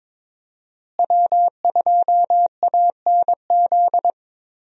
Morse code, heard as a keyed tone, about 22 wpm, 700 Hz. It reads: W2ANZ
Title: USA Call Signs - 5 Characters - Repeated 3x - 22wpm